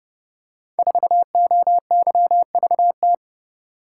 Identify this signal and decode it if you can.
Morse code, heard as a keyed tone, about 30 wpm, 700 Hz. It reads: 4OYVT
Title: Code Groups - 5 Letter:Numbers - 30wpm